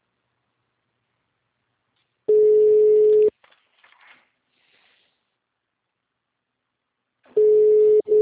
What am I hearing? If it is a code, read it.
Ringback tone (Europe, 425 Hz).